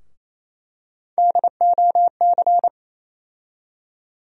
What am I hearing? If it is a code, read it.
Morse code, heard as a keyed tone, about 28 wpm, 700 Hz. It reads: DOC
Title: USA Call Sign Postfixes - 3 Letters 28wpm